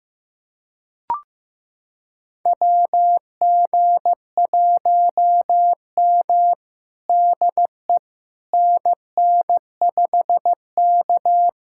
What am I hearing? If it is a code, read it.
Morse code, heard as a keyed tone, about 15 wpm, 700 Hz. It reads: WG1M DE NN5K
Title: USA Callsign DE Callsign - 4 Characters - 15wpm